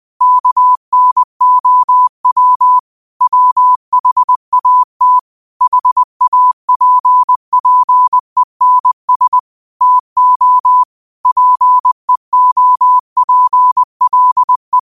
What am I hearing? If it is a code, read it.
Morse code, heard as a keyed tone, about 20 wpm, 1000 Hz. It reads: KNOW WHAT HAPPENS TO PEOPLE